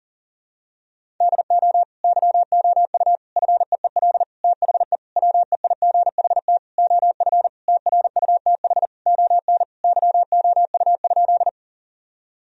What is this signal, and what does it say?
Morse code, heard as a keyed tone, about 40 wpm, 700 Hz. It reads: DO YOU FEEL THE WEIGHT OF TRUTH ON YOU?
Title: Sentences from Top 1000 Words - Review Part 3 - 40wpm